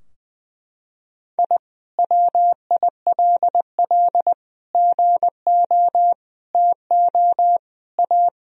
Morse code, heard as a keyed tone, about 20 wpm, 700 Hz. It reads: I WILL GO TO A
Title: Sentences from Top 1000 Words - Review Part 4 - 20wpm